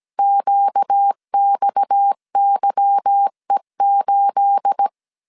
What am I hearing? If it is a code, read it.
Morse code, heard as a keyed tone, about 17 wpm, 800 Hz. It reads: QXYE8
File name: CW / DET / 1 9 1500C S A 16wpm 2828